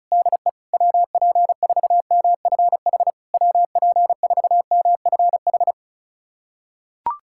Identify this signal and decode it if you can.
Morse code, heard as a keyed tone, about 36 wpm, 700 Hz. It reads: DE WP4MFH WP4MFH
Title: SOTA QSO Practice 35wpm